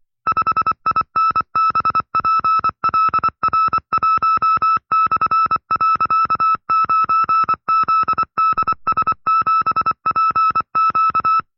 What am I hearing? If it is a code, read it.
Morse code, heard as a keyed tone, about 24 wpm, 1310 Hz. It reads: 5INBPLR1/.9ZDS7PQ
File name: FreeFiles / CW-practice / 20WPM-side2